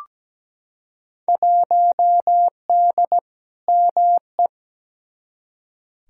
Morse code, heard as a keyed tone, about 17 wpm, 700 Hz. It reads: 1D ME
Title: Field Day Exchanges - 17wpm